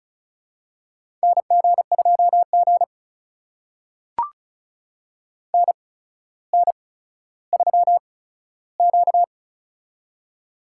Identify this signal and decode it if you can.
Morse code, heard as a keyed tone, about 36 wpm, spaced as for 15 wpm, 700 Hz. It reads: NG2G NN3Q
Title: USA Call Signs - 4 Characters - 8x Character Spacing - 35wpm